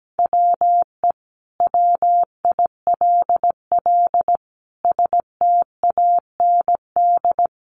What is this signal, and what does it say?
Morse code, heard as a keyed tone, about 17 wpm, 700 Hz. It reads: WE WILL STAND
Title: Sentences from Top 1400 Words 17wpm